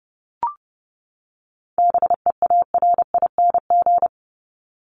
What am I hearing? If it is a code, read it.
Morse code, heard as a keyed tone, about 30 wpm, 700 Hz. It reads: BEARING
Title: ING Suffix - Top 100 Words with Base Words 30wpm